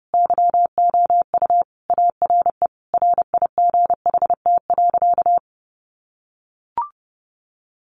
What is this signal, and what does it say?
Morse code, heard as a keyed tone, about 30 wpm, 700 Hz. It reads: YOU ARE RIGHT.
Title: Sentences from Top 500 Words 30wpm